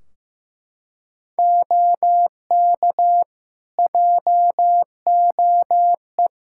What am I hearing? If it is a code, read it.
Morse code, heard as a keyed tone, about 15 wpm, 700 Hz. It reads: OK JOE
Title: Practice QSO - Exchange 3 - More Info - 15wpm